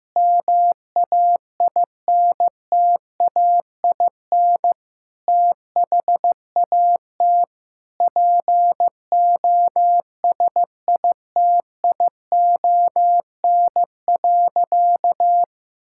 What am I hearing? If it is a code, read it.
Morse code, heard as a keyed tone, about 15 wpm, 700 Hz. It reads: MAINTAIN THAT POSITION.
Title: Sentences from Top 1000 Words - Review Part 5 - 15wpm